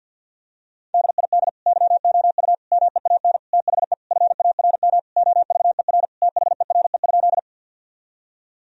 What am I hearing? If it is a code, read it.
Morse code, heard as a keyed tone, about 51 wpm, 700 Hz. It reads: DID YOU MEAN THE FARM OVER THERE?
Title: Sentences from Top 1000 Words - Review Part 2 - 50wpm